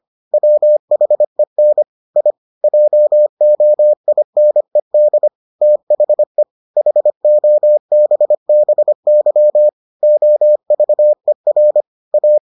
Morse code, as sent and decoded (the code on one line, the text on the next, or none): .-- .... -. / .. / .--- --- .. -. -.. / - .... / .... --- -... -... -.-- / --- ...- . .-. / .-
WHEN I JOINED THE HOBBY OVER A